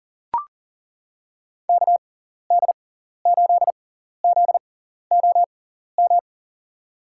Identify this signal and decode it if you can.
Morse code, heard as a keyed tone, about 40 wpm, spaced as for 14 wpm, 700 Hz. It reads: KD8ZOM
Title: USA Call Signs - 6 Characters - 6x Character Spacing - 40wpm